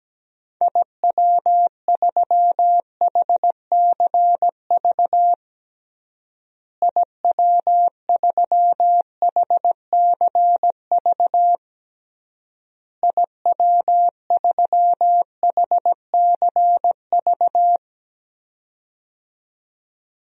Morse code, heard as a keyed tone, about 17 wpm, 700 Hz. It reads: IW3HCV IW3HCV IW3HCV
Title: DXCC-100 Call Signs - 6 Characters - Repeated 3x - 17wpm